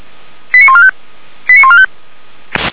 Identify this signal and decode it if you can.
Five-tone selective call (ZVEI), heard as quick runs of five tones, tones of about 70 ms. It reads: 89145 89145